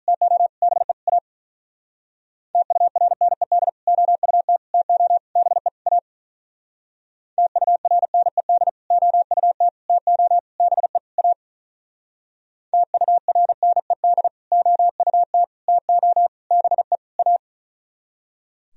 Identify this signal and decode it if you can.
Morse code, heard as a keyed tone, about 42 wpm, 700 Hz. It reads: TO BE A TURNED OUT TO BE A TURNED OUT TO BE A TURNED OUT TO BE A
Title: Speed-Racing - Sets of 5 Words 35wpm